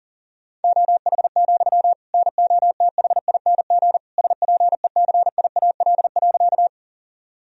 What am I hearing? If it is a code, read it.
Morse code, heard as a keyed tone, about 40 wpm, 700 Hz. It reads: OH, NOTHING SPECIAL.
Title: Sentences from Top 1000 Words - Review Part 3 - 40wpm